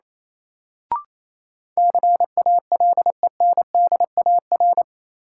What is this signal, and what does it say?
Morse code, heard as a keyed tone, about 28 wpm, 700 Hz. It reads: CALENDAR
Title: Sentences from Top 1600 Words - New Word Drill Alt - 28wpm